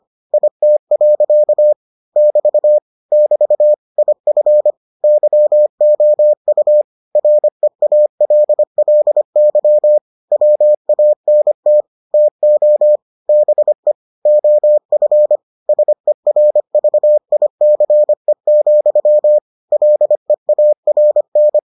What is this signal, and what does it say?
Morse code, heard as a keyed tone, about 25 wpm, 600 Hz. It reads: IT. <BT> <BT> IF YOU REALLY WANT TO BE OF SERVICE, LEARN